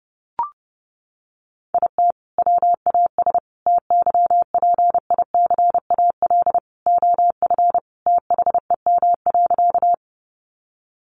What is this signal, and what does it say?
Morse code, heard as a keyed tone, about 30 wpm, 700 Hz. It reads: IT WAS TYPICAL OF THEM.
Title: Sentences from Top 1100 Words 30wpm